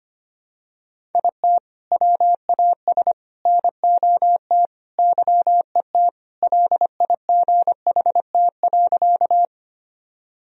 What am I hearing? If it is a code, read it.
Morse code, heard as a keyed tone, about 25 wpm, 700 Hz. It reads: IT WAS NOT YET LIGHT.
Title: Sentences from Top 1000 Words - Review Part 1 - 25wpm